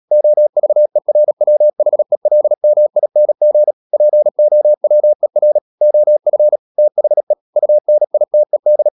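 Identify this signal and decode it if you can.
Morse code, heard as a keyed tone, about 37 wpm, 600 Hz. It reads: OVERWHELMING POWER OF THE UNITED
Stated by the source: Morse code practice at 35–39 wpm sending English text of Ken Follett's novel Never, from CW med Gnister.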